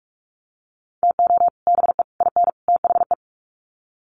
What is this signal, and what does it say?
Morse code, heard as a keyed tone, about 44 wpm, 700 Hz. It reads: TO BE IN THE
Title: Sets of 4 Words 45wpm